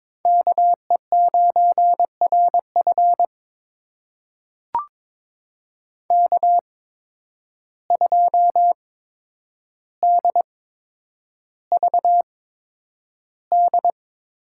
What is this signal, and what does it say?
Morse code, heard as a keyed tone, about 22 wpm, spaced as for 9 wpm, 700 Hz. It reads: KE9RF K2DVD
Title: USA Call Signs - 5 Characters - 8x Character Spacing - 22wpm